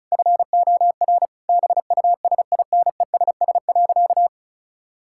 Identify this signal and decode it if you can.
Morse code, heard as a keyed tone, about 34 wpm, 700 Hz. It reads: FOR BUSINESS.
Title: Sentences from Top 1000 Words - Review Part 6 - 35wpm